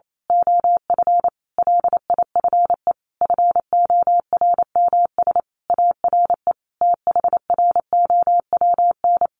Morse code, heard as a keyed tone, about 28 wpm, 700 Hz. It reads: OF LIFE FORMS ARE THROWN
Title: A Sound Of Thunder 28wpm